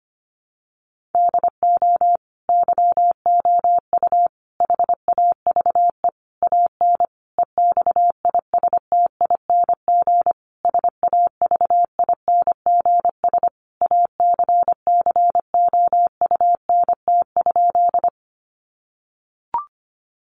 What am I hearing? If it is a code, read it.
Morse code, heard as a keyed tone, about 25 wpm, 700 Hz. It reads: DO YOU HAVE AN EXISTING SAVINGS ACCOUNT?
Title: Sentences from Top 1500 Words 25wpm